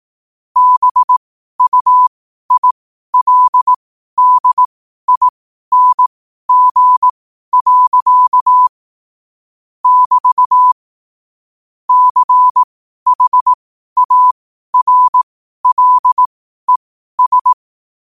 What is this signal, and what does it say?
Morse code, heard as a keyed tone, about 18 wpm, spaced as for 12 wpm, 1000 Hz. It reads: BUILDING. <BT> CHARLES
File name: qotdfiles / 12wpm